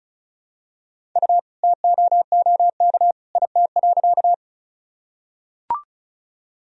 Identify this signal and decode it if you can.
Morse code, heard as a keyed tone, about 34 wpm, 700 Hz. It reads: U TOOK IT.
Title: Sentences from Top 300 Words and Abbreviations 35wpm